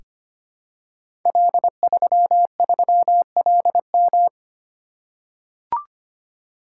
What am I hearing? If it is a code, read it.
Morse code, heard as a keyed tone, about 25 wpm, 700 Hz. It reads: L33LM
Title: Code Groups - 5 Letter:Numbers - 25wpm